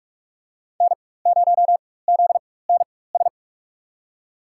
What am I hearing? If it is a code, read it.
Morse code, heard as a keyed tone, about 44 wpm, spaced as for 21 wpm, 700 Hz. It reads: N0ZNS
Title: USA Call Signs - 4x Character Spacing - 45wpm